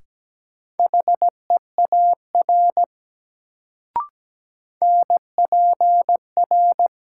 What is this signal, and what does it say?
Morse code, heard as a keyed tone, about 17 wpm, 700 Hz. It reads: HEAR NPR
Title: Sets of 5 Words - New Word Drill - 17wpm